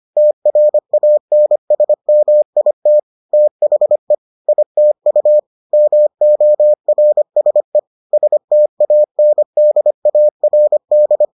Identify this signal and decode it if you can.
Morse code, heard as a keyed tone, about 25 wpm, 600 Hz. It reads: TRANSMIT THE ITU MORSE STANDARD